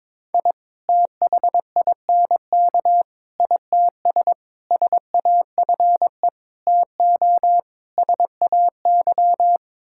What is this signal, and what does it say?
Morse code, heard as a keyed tone, about 22 wpm, 700 Hz. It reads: I THINK ITS SAFE TO SAY
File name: practice / 22wpm / Sets of 6 Words - 22wpm